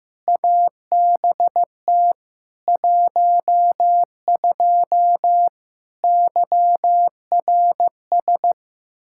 Morse code, heard as a keyed tone, about 15 wpm, 700 Hz. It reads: ABT 12 YRS
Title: Practice QSO - Exchange 3 - More Info - 15wpm